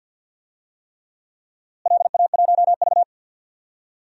Morse code, heard as a keyed tone, about 52 wpm, 700 Hz. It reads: LA1V
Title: DXCC-100 Call Signs - 4 Characters - Repeated 3x - 50wpm